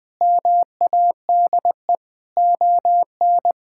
Morse code, heard as a keyed tone, about 21 wpm, 700 Hz. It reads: MADE ON
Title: Sentences with 5-Word Phrases - 20wpm